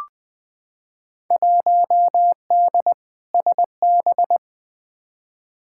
Morse code, heard as a keyed tone, about 20 wpm, 700 Hz. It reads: 1D SB
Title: Field Day Exchanges - 20wpm